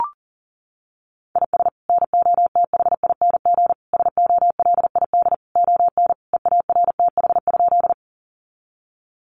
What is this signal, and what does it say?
Morse code, heard as a keyed tone, about 40 wpm, 700 Hz. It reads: IS NOTHING SOLID ON EARTH?
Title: Sentences from Top 1500 Words 40wpm